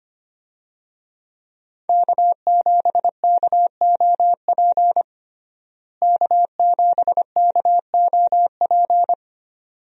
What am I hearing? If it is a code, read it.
Morse code, heard as a keyed tone, about 25 wpm, 700 Hz. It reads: K7KOP K7KOP
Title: USA Call Signs - 5 Characters - Repeated 2x - 25wpm